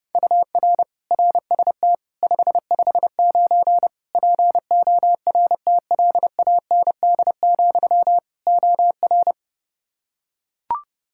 Morse code, heard as a keyed tone, about 32 wpm, 700 Hz. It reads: UR RST 559 PORTLAND, OR
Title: K1USN SST QSO Practice 30wpm